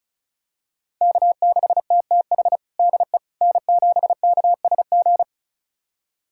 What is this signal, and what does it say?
Morse code, heard as a keyed tone, about 36 wpm, 700 Hz. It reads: K6TTH DE N7KSG
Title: USA Callsign DE Callsign - 5 Characters - 35wpm